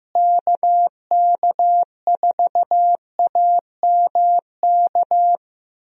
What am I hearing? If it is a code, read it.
Morse code, heard as a keyed tone, about 15 wpm, 700 Hz. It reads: KK4AMK
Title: USA Call Signs - 6 Characters - Repeated 3x - 15wpm